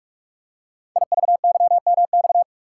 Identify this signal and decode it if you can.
Morse code, heard as a keyed tone, about 44 wpm, 690 Hz. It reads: IVYMX
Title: Code Groups - 5 Letters - 45wpm